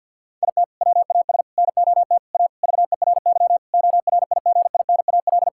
Morse code, heard as a keyed tone, about 50 wpm, 700 Hz. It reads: IT WAS NOT A VERY ORIGINAL
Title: Sentences from Top 1000 Words - Review Part 6 - 50wpm